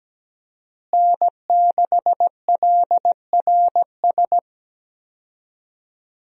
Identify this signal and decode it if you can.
Morse code, heard as a keyed tone, about 17 wpm, 700 Hz. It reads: N6LRS
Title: USA Call Signs - 5 Characters - Repeated 3x - 17wpm